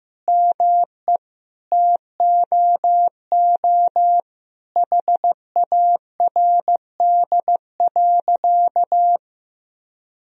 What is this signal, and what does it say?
Morse code, heard as a keyed tone, about 15 wpm, 700 Hz. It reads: ME TOO HARD.
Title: Sentences from Top 1000 Words - Review Part 1 - 15wpm